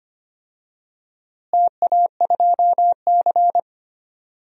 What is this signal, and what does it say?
Morse code, heard as a keyed tone, about 25 wpm, 700 Hz. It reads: TA2C
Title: DXCC-100 Call Signs - 4 Characters - Repeated 3x - 25wpm